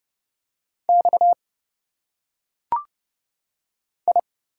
Morse code, heard as a keyed tone, about 30 wpm, 700 Hz. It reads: X I
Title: Single Letters - Phonetic Pronunciation 30wpm